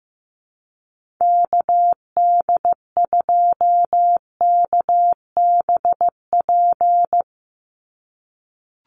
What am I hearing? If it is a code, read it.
Morse code, heard as a keyed tone, about 15 wpm, 700 Hz. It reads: KD2KBP
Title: USA Call Signs 15wpm